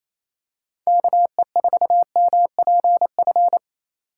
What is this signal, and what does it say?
Morse code, heard as a keyed tone, about 28 wpm, 700 Hz. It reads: KE4MPF
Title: USA Call Signs - 6 Characters - 4x Character Spacing - 28wpm